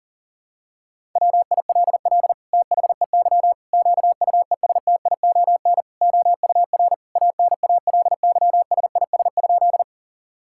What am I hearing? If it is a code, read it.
Morse code, heard as a keyed tone, about 40 wpm, 700 Hz. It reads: WILL THEY QUESTION OUR ANALYSIS?
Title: Sentences from Top 1000 Words - Review Part 5 - 40wpm